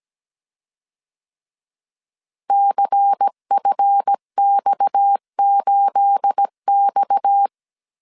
Morse code, heard as a keyed tone, about 17 wpm, 800 Hz. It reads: CFX8X
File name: CW / DET / 1 9 1500C S A 16wpm 2828